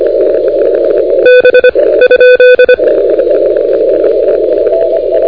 Morse code, heard as a keyed tone, about 25 wpm, 1500 Hz, with noise. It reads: B ?
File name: recordings / NU1O de NT1K ARRL10M CW 28MHz